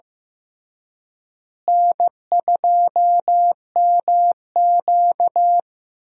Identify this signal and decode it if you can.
Morse code, heard as a keyed tone, about 15 wpm, 700 Hz. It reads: N2MQ
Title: USA Call Signs - 4 Characters - Repeated 3x - 15wpm